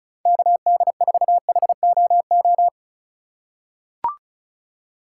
Morse code, heard as a keyed tone, about 34 wpm, 700 Hz. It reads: KD4HOO
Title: USA Call Signs - 2x Character Spacing - 35wpm